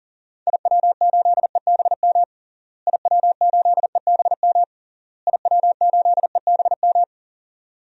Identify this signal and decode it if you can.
Morse code, heard as a keyed tone, about 37 wpm, 700 Hz. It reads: IW8EBM IW8EBM IW8EBM
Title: DXCC-100 Call Signs - 6 Characters - Repeated 3x - 40wpm